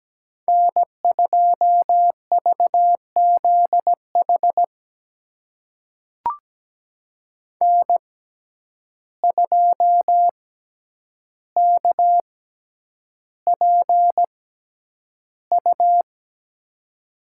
Morse code, heard as a keyed tone, about 17 wpm, spaced as for 9 wpm, 700 Hz. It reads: N2VZH N2KPU
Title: USA Call Signs - 6x Character Spacing - 17wpm